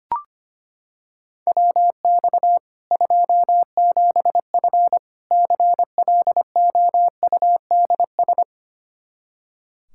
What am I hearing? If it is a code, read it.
Morse code, heard as a keyed tone, about 25 wpm, 700 Hz. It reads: WX 27F CLOUDS
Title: Practice QSO - Exchange 2 - Info - 25wpm